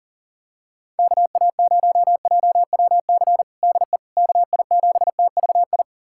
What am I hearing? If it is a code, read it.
Morse code, heard as a keyed tone, about 37 wpm, 700 Hz. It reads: KA0JWC DE KI7TVI
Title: USA Callsign DE Callsign - 6 Characters - 40wpm